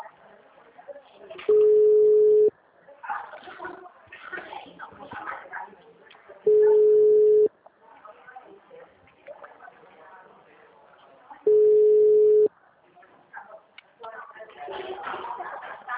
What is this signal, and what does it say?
Ringback tone (Europe, 425 Hz), with speech.